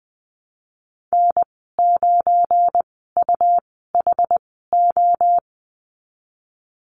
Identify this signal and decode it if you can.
Morse code, heard as a keyed tone, about 20 wpm, spaced as for 15 wpm, 700 Hz. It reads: N9UHO
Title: 179 Lesson 29g - Slash - Call Signs 20wpm